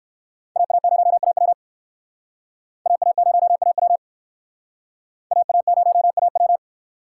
Morse code, heard as a keyed tone, about 60 wpm, 700 Hz. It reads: AA0AW AA0AW AA0AW